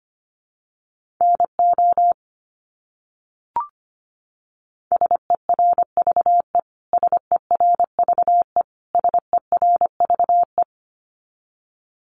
Morse code, heard as a keyed tone, about 25 wpm, 700 Hz. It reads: NO SERVE SERVE SERVE
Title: Top 500 Words Repeated 25wpm